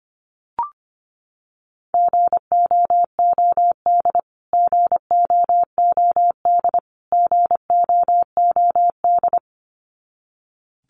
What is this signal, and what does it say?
Morse code, heard as a keyed tone, about 25 wpm, 700 Hz. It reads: GOOD GOOD GOOD
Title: Top 300 Words Repeated 25wpm